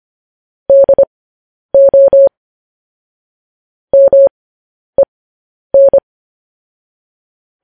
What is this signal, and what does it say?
Morse code, heard as a keyed tone, about 25 wpm, spaced as for 10 wpm, 550 Hz. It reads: DO MEN